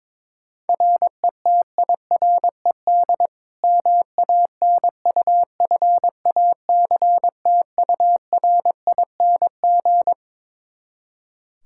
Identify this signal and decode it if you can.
Morse code, heard as a keyed tone, about 22 wpm, 700 Hz. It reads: RETIRED MANUFACTURING
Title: K1USN SST QSO Practice 22wpm